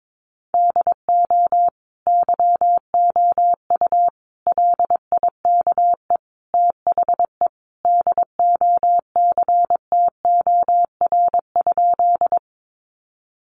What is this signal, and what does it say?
Morse code, heard as a keyed tone, about 22 wpm, 700 Hz. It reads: DO YOU LIKE THE DOCTOR?
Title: Sentences from Top 1500 Words 22wpm